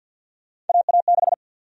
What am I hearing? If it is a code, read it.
Morse code, heard as a keyed tone, about 52 wpm, 700 Hz. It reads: AA6
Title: USA Call Sign Prefixes - 50wpm